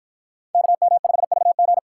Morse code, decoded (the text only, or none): KM5VG